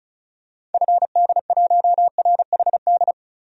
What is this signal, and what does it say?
Morse code, heard as a keyed tone, about 34 wpm, 700 Hz. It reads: FD1RHD
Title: DXCC-100 Call Signs - 6 Characters - Repeated 2x - 35wpm